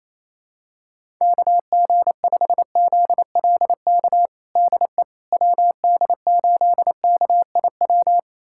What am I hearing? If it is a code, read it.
Morse code, heard as a keyed tone, about 28 wpm, 700 Hz. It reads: KG5ZLK DE WD8KIW
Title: USA Callsign DE Callsign - 6 Characters - 28wpm